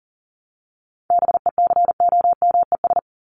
Morse code, heard as a keyed tone, about 40 wpm, 700 Hz. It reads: BECOMES
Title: ES:S Suffix - Top 100 Words with Base Words 40wpm